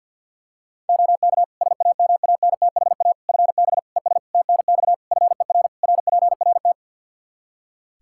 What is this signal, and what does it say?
Morse code, heard as a keyed tone, about 51 wpm, 700 Hz. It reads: OK SAMANTHA FB ES TNX FER RPRT
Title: Practice QSO - Exchange 2 - Info - 50wpm